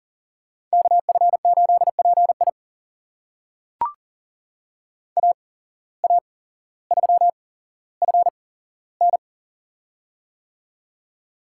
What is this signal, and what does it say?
Morse code, heard as a keyed tone, about 40 wpm, spaced as for 17 wpm, 700 Hz. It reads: KF8PI AA3FN
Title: USA Call Signs - 5 Characters - 8x Character Spacing - 40wpm